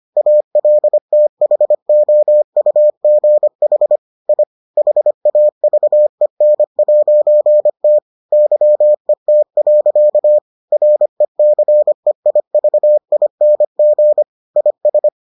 Morse code, heard as a keyed tone, about 25 wpm, 600 Hz. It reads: ALTHOUGH I HAVEN'T YET. RECEIVING IS